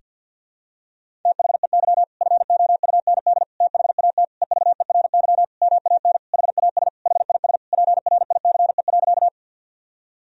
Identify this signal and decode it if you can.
Morse code, heard as a keyed tone, about 50 wpm, 700 Hz. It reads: THEY FOUND THAT EVERY MAN HAS HIS PRICE.
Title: Sentences from Top 1000 Words - Review Part 4 - 50wpm